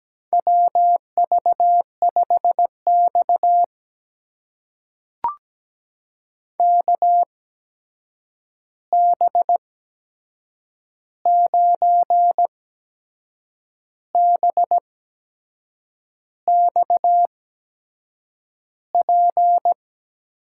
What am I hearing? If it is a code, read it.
Morse code, heard as a keyed tone, about 17 wpm, spaced as for 6 wpm, 700 Hz. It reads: WV5X KB9BXP
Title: USA Call Signs - 8x Character Spacing - 17wpm